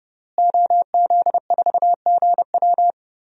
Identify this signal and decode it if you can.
Morse code, heard as a keyed tone, about 30 wpm, 700 Hz. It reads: OZ4GW
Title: DXCC-100 Call Signs - 5 Characters - Repeated 2x - 30wpm